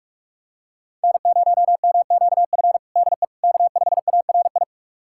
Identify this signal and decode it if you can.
Morse code, heard as a keyed tone, about 46 wpm, 700 Hz. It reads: N0MQF DE K5ARI
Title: USA Callsign DE Callsign 45wpm